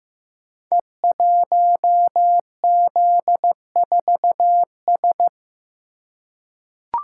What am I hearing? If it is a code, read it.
Morse code, heard as a keyed tone, about 15 wpm, 700 Hz. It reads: E1Z4S
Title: Code Groups - 5 Letter:Numbers - 15wpm